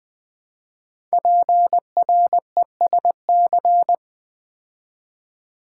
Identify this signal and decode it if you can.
Morse code, heard as a keyed tone, about 20 wpm, 700 Hz. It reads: PRESC
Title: Build A Word - 5 Letters At A Time - 12-Letter Words - 20wpm